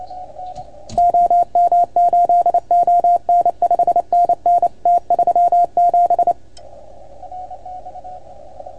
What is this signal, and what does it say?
Morse code, heard as a keyed tone, about 28 wpm, 690 Hz. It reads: OM8ON 5NN T37
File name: Archiv / CUC310 / OK1FGD CUC 310 pileup